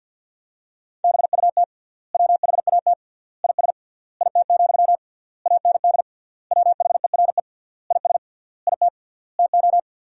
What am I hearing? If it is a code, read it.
Morse code, heard as a keyed tone, about 50 wpm, 700 Hz. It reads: BUT WHAT IS IT, AND WHERE IS IT TO
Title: 3x Word-Spacing - Sentences from Top 300 Words - 50wpm